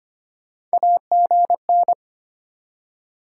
Morse code, heard as a keyed tone, about 33 wpm, 700 Hz. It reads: AGN